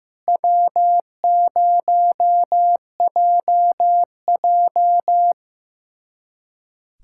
Morse code, heard as a keyed tone, about 16 wpm, 700 Hz. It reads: W0JJ
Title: USA Callsign DE Callsign - 4 Characters - 15wpm